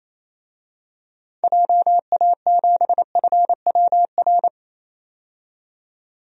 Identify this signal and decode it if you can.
Morse code, heard as a keyed tone, about 28 wpm, 700 Hz. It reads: JA7FWR
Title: DXCC-100 Call Signs - 6 Characters - 28wpm